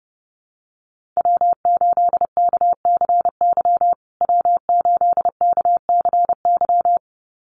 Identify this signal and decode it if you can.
Morse code, heard as a keyed tone, about 30 wpm, 700 Hz. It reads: W8KCY W8KCY
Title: USA Call Signs Repeated 2x 30wpm